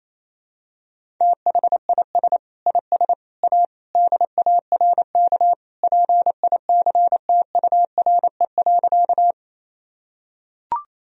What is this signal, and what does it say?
Morse code, heard as a keyed tone, about 28 wpm, 700 Hz. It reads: THIS IS A DARK PICTURE.
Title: Sentences from Top 1000 Words - Review Part 1 - 28wpm